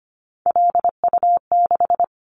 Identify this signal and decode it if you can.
Morse code, heard as a keyed tone, about 25 wpm, 700 Hz. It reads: LU6
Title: Callsign Prefixes for Top 50 DXCC - 25wpm